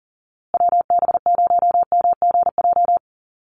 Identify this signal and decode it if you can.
Morse code, heard as a keyed tone, about 40 wpm, 700 Hz. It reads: WB0MGJ
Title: USA Call Signs Repeated 2x 40wpm